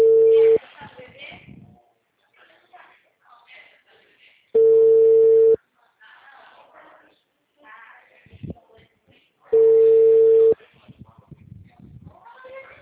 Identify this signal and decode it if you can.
Ringback tone (China, 450 Hz), with speech.